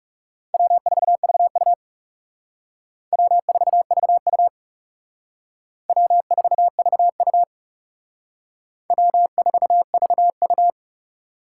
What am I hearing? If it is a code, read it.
Morse code, heard as a keyed tone, about 36 wpm, 700 Hz. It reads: W4VU W4VU W4VU W4VU
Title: Speed Racing - USA 1x2 Call Signs 30wpm